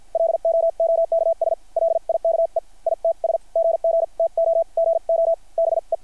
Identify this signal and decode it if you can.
Morse code, heard as a keyed tone, about 50 wpm, spaced as for 42 wpm, 660 Hz. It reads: LOOKS LIKE ITS GGTOGO BE
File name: archived / CRSnet